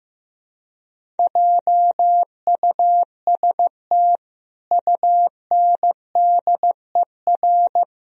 Morse code, heard as a keyed tone, about 16 wpm, 700 Hz. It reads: JUST UNDER
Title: Sentences from Top 1000 Words - Review Part 1 - 15wpm